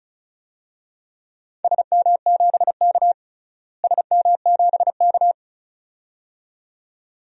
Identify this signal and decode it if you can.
Morse code, heard as a keyed tone, about 36 wpm, 700 Hz. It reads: SM7K SM7K
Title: DXCC-100 Call Signs - 4 Characters - Repeated 2x - 35wpm